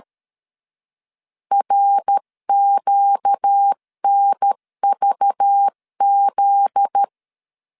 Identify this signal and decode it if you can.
Morse code, heard as a keyed tone, about 13 wpm, 800 Hz. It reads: RQNVZ
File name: LEZIONI / 7 GF9Z / 7 GF9Z 12wpm 2147